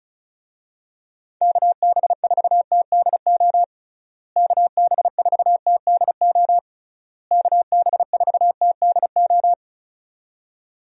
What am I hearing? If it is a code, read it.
Morse code, heard as a keyed tone, about 34 wpm, 700 Hz. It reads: KB4TDO KB4TDO KB4TDO
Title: USA Call Signs - 6 Characters - Repeated 3x - 35wpm